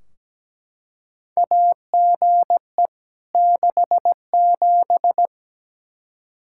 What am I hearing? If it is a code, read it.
Morse code, heard as a keyed tone, about 17 wpm, 700 Hz. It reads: AGE 67
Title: Practice QSO - Exchange 3 - More Info - 17wpm